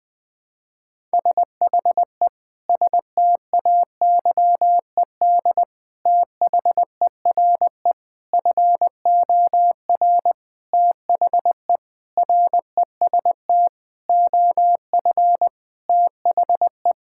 Morse code, heard as a keyed tone, about 20 wpm, 700 Hz. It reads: SHE STAYED THERE FOR THE REST OF THE
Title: Sentences with 5-Word Phrases - 20wpm